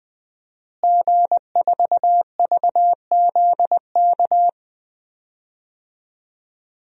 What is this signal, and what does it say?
Morse code, heard as a keyed tone, about 20 wpm, 700 Hz. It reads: G4VZK
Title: DXCC-100 Call Signs - 5 Characters - Repeated 2x - 20wpm